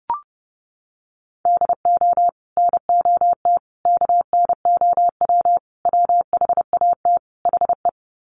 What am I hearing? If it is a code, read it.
Morse code, heard as a keyed tone, about 30 wpm, 700 Hz. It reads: DO NOT KNOW WHAT HE
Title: Sets of 5 Words Encore2 30wpm